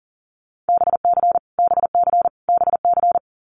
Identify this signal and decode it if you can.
Morse code, heard as a keyed tone, about 40 wpm, 700 Hz. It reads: BC BC BC
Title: Canadian Provinces and Territories - Repeated 3x - 40wpm